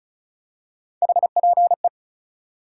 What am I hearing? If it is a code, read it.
Morse code, heard as a keyed tone, about 36 wpm, 700 Hz. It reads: HPE
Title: Top 180 QSO Elements 35wpm